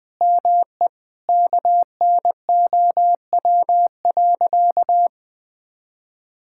Morse code, heard as a keyed tone, about 20 wpm, 700 Hz. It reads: ME KNOW.
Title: Sentences with 8-Word Phrases - Spoken Beginning - 20wpm